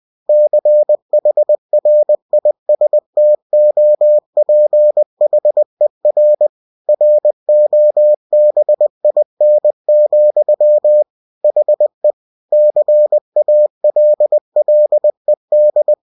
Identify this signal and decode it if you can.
Morse code, heard as a keyed tone, about 20 wpm, 600 Hz. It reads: CHRISTOPHER ROBIN, HE CALLED
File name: cw-books / winnie-the-pooh / winnie-20wpm-page-0076